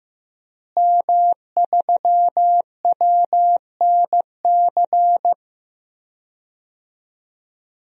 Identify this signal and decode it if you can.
Morse code, heard as a keyed tone, about 15 wpm, 700 Hz. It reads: M3WNC
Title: DXCC-100 Call Signs - 5 Characters - Repeated 2x - 15wpm